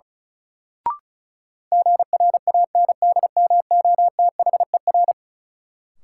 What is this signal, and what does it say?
Morse code, heard as a keyed tone, about 36 wpm, 700 Hz. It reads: GRANDMOTHER
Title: Sentences from Top 1200 Words - New Word Drill Alt - 35wpm